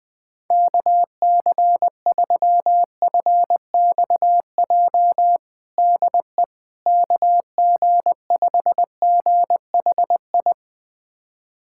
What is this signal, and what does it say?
Morse code, heard as a keyed tone, about 21 wpm, 690 Hz. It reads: KC3FXJ DE KG5GHI
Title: USA Callsign DE Callsign 20wpm